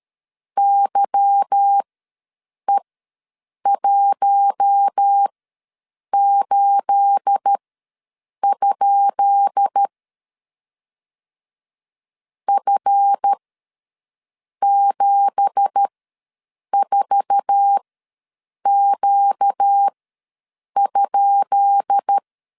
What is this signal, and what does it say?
Morse code, heard as a keyed tone, about 13 wpm, spaced as for 7 wpm, 800 Hz. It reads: YE18? F74Q?
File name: LEZIONI / 10 SIGNS / 10B INT ws 12wpm 3527